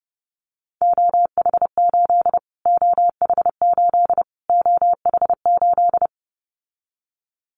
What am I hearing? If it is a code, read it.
Morse code, heard as a keyed tone, about 30 wpm, 700 Hz. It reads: OH8 OH8 OH8
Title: Callsign Prefixes for Top 25 DXCC - Repeated 3x - 30wpm